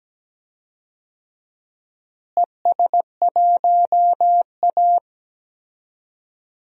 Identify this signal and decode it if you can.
Morse code, heard as a keyed tone, about 17 wpm, 700 Hz. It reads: ES1A
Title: DXCC-100 Call Signs - 4 Characters - Repeated 3x - 17wpm